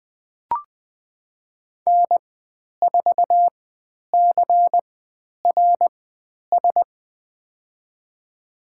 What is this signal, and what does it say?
Morse code, heard as a keyed tone, about 20 wpm, spaced as for 10 wpm, 700 Hz. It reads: N4CRS
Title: 102 Lesson 17g - Letter u - Call Signs 20wpm